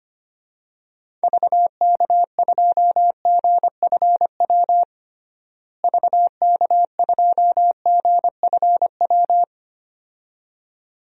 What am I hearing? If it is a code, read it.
Morse code, heard as a keyed tone, about 25 wpm, 700 Hz. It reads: VK2GFW VK2GFW
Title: DXCC-100 Call Signs - 6 Characters - Repeated 2x - 25wpm